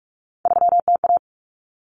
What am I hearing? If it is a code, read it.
Morse code, heard as a keyed tone, about 44 wpm, 700 Hz. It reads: 3TA
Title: Common Cut Numbers - 3-Digit 45wpm